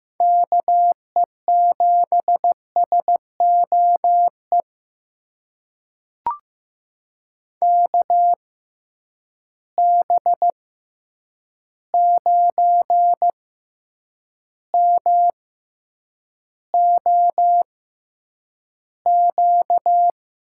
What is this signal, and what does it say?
Morse code, heard as a keyed tone, about 15 wpm, spaced as for 8 wpm, 700 Hz. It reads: KE7SOE KB9MOQ
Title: USA Call Signs - 6 Characters - 6x Character Spacing - 15wpm